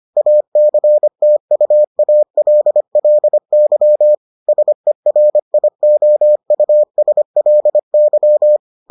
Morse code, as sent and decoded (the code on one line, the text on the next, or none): .- -.-. - ..- .- .-.. .-.. -.-- / ... .-. .. --- ..- ... .-.. -.--
ACTUALLY SERIOUSLY